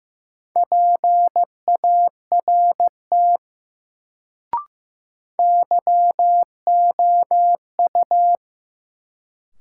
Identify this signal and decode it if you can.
Morse code, heard as a keyed tone, about 15 wpm, 700 Hz. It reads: PART YOU
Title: Sets of 2 Words - New Word Drill - 15wpm